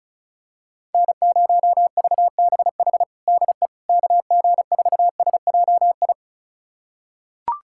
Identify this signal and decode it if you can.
Morse code, heard as a keyed tone, about 36 wpm, 700 Hz. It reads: N0VBH DE KG4SJI
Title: USA Callsign DE Callsign 35wpm